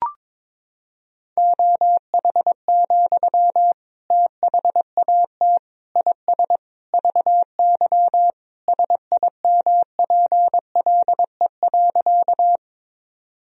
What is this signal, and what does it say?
Morse code, heard as a keyed tone, about 22 wpm, 700 Hz. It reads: OH, THAT IS VY SIMPLE.
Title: Sentences from Top 500 Words and Abbreviations 22wpm